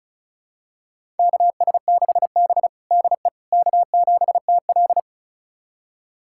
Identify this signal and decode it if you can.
Morse code, heard as a keyed tone, about 36 wpm, 700 Hz. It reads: KS6B DE K7TL
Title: USA Callsign DE Callsign - 4 Characters - 35wpm